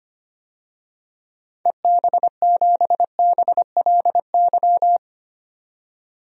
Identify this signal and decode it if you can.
Morse code, heard as a keyed tone, about 25 wpm, 700 Hz. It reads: EB7BLY
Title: DXCC-100 Call Signs - 6 Characters - 25wpm